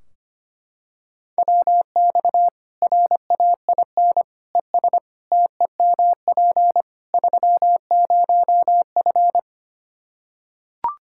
Morse code, heard as a keyed tone, about 25 wpm, 700 Hz. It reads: WX RAIN ES TEMP 30F
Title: K1USN SST QSO Practice 25wpm